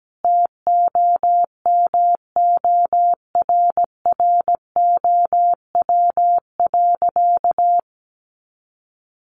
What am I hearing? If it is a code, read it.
Morse code, heard as a keyed tone, about 17 wpm, 700 Hz. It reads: TOMORROW.
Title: Additional Easy Phrases 17wpm